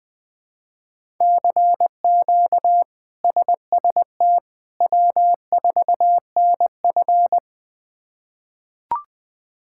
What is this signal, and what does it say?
Morse code, heard as a keyed tone, about 20 wpm, 700 Hz. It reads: CQ SST W4NF
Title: K1USN SST QSO Practice - 20wpm